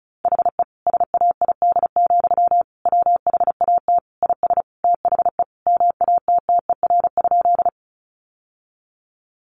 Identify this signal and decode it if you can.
Morse code, heard as a keyed tone, about 34 wpm, 700 Hz. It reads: HE SAID, WHAT IS THE MATTER?
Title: Sentences from Top 700 Words 35wpm